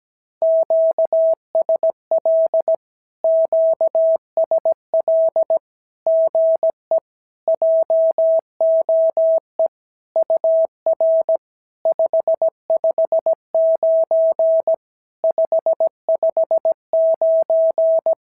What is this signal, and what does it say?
Morse code, heard as a keyed tone, about 16 wpm, 650 Hz. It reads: QSL QSL GE JOE UR 559 559
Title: SOTA QSO Practice 17wpm